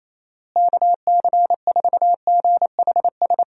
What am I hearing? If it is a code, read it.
Morse code, heard as a keyed tone, about 28 wpm, 700 Hz. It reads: KC4GHS
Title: USA Call Signs - 6 Characters - Repeated 2x - 28wpm